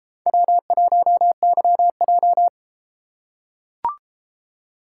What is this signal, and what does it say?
Morse code, heard as a keyed tone, about 33 wpm, 700 Hz. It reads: W1YJ